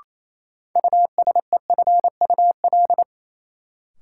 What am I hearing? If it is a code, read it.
Morse code, heard as a keyed tone, about 28 wpm, 700 Hz. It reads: USEFUL
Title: Sentences from Top 700 Words - New Word Drill - 28wpm